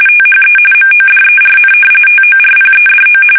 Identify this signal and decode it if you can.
RTTY, about 45.45 baud. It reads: THE MODERN TORPEDO
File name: fsk850 / usnavy3